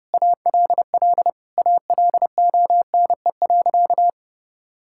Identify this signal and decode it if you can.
Morse code, heard as a keyed tone, about 30 wpm, 700 Hz. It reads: ALL ALONE.
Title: Sentences from Top 1000 Words - Review Part 5 - 30wpm